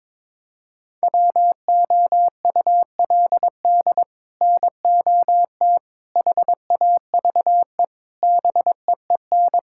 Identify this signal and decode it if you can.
Morse code, heard as a keyed tone, about 22 wpm, 700 Hz. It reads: WOULD NOT HAVE BEEN
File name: practice / 22wpm / Sets of 6 Words - 22wpm